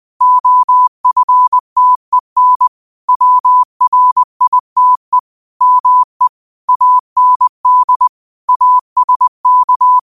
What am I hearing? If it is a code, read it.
Morse code, heard as a keyed tone, about 20 wpm, 1000 Hz. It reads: OFTEN WRITE ME AND ASK
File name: qotdfiles / 20wpm